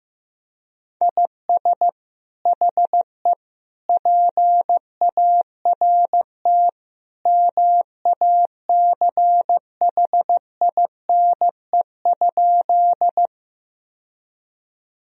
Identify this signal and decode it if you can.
Morse code, heard as a keyed tone, about 16 wpm, 700 Hz. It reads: IS HE PART MACHINE?
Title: Sentences from Top 1000 Words - Review Part 2 - 15wpm